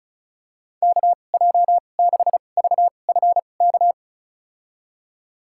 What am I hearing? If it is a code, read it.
Morse code, heard as a keyed tone, about 34 wpm, spaced as for 25 wpm, 700 Hz. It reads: KJ6VFK
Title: USA Call Signs - 6 Characters - 2x Character Spacing - 35wpm